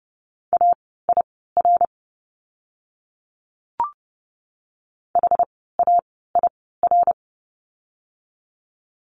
Morse code, heard as a keyed tone, about 30 wpm, spaced as for 17 wpm, 700 Hz. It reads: AIR HAIR
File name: Lesson 17f - Letter u - Words 30wpm